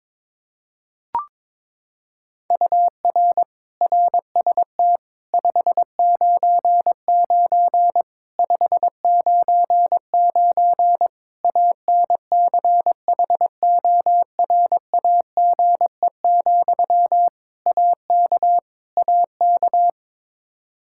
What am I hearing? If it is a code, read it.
Morse code, heard as a keyed tone, about 22 wpm, 700 Hz. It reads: UR RST 599 599 ANCHORAGE, AK AK
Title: Practice QSO - Exchange 1 - Beginning - 22wpm